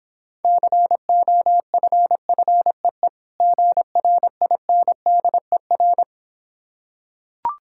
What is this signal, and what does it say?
Morse code, heard as a keyed tone, about 26 wpm, 700 Hz. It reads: COFFEE GRINDER